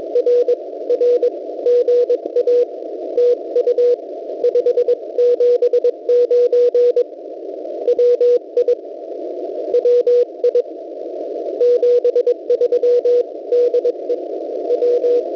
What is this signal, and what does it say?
Morse code, heard as a keyed tone, about 22 wpm, 490 Hz, with radio noise. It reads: R R GA TU 579 WI WI 73 DE W
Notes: noise